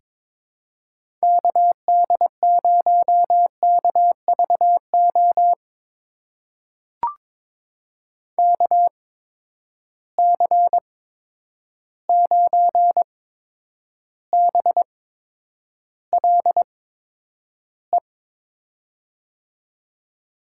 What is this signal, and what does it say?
Morse code, heard as a keyed tone, about 22 wpm, spaced as for 9 wpm, 700 Hz. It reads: KD0KVO KC9BLE
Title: USA Call Signs - 8x Character Spacing - 22wpm